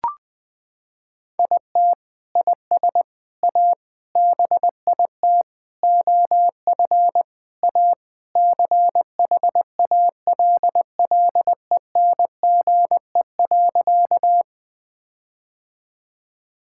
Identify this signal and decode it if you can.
Morse code, heard as a keyed tone, about 20 wpm, 700 Hz. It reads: IT IS A BIT OF A CHALLENGE.
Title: Sentences with 4-Word Phrases - 20wpm